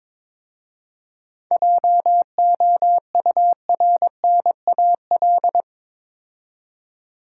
Morse code, heard as a keyed tone, about 22 wpm, 700 Hz. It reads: JOURNAL
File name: practice / 22wpm / Difficult Words with Hints - 22wpm